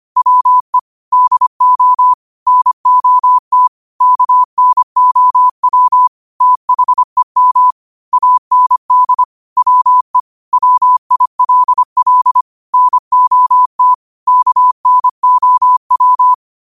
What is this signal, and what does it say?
Morse code, heard as a keyed tone, about 25 wpm, 1000 Hz. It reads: WE DO NOT KNOW THEM AND WE WILL NOT KNOW
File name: qotdfiles / 25wpm